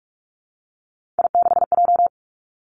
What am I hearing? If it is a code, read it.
Morse code, heard as a keyed tone, about 46 wpm, 700 Hz. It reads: I6J
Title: Three Challenging Letter-Number Combinations 45wpm